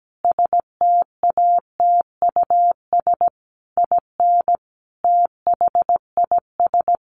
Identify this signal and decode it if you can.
Morse code, heard as a keyed tone, about 17 wpm, 700 Hz. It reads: STATUS IN THIS
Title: Sentences from Top 1400 Words 17wpm